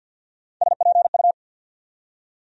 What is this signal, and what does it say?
Morse code, heard as a keyed tone, about 50 wpm, 700 Hz. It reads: SPU
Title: USA Call Sign Postfixes - 3 Letters 50wpm